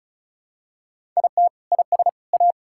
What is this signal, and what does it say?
Morse code, heard as a keyed tone, about 34 wpm, 700 Hz. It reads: IT IS A
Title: Sentences from Top 1000 Words - Review Part 5 - 35wpm